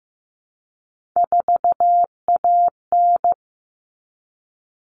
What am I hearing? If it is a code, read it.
Morse code, heard as a keyed tone, about 15 wpm, 700 Hz. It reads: 4AN
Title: All Cut Numbers - 3-Digit 15wpm